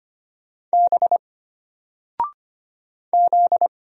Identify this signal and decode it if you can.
Morse code, heard as a keyed tone, about 25 wpm, 700 Hz. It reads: B Z
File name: Single-Letter-Number-25wpm